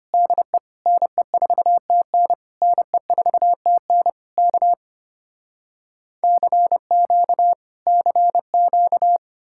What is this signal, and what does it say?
Morse code, heard as a keyed tone, about 33 wpm, 700 Hz. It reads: DE NE4TN NE4TN K CQ CQ